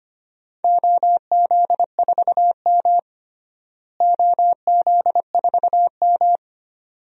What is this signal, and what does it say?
Morse code, heard as a keyed tone, about 25 wpm, 700 Hz. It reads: OZ4M OZ4M
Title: DXCC-100 Call Signs - 4 Characters - Repeated 2x - 25wpm